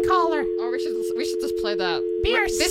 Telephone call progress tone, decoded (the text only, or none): dial tone